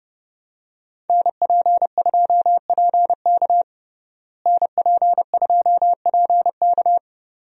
Morse code, heard as a keyed tone, about 30 wpm, 700 Hz. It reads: NP2PK NP2PK
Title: USA Call Signs - 5 Characters - Repeated 2x - 30wpm